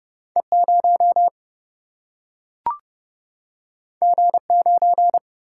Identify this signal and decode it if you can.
Morse code, heard as a keyed tone, about 29 wpm, 700 Hz. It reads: E 0 G 9
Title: Single-Letter Single-Number 30wpm